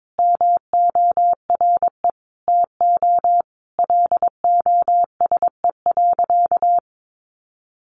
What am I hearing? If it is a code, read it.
Morse code, heard as a keyed tone, about 22 wpm, 700 Hz. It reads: MORE TO LOSE.
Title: Sentences from Top 1100 Words 22wpm